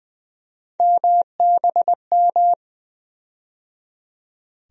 Morse code, heard as a keyed tone, about 21 wpm, 700 Hz. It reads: MBM
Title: USA Call Sign Postfixes - 3 Letters 20wpm